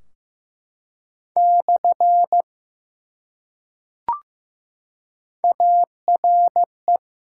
Morse code, heard as a keyed tone, about 15 wpm, 700 Hz. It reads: / ARE
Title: Top 180 QSO Elements 15wpm